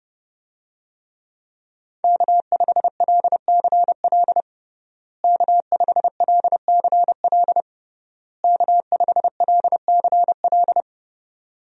Morse code, heard as a keyed tone, about 30 wpm, 700 Hz. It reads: K5LCL K5LCL K5LCL
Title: USA Call Signs - 5 Characters - Repeated 3x - 30wpm